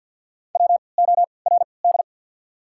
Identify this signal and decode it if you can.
Morse code, heard as a keyed tone, about 53 wpm, 690 Hz. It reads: W O R D
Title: 3x Character Spacing - Top 200 Words - 50wpm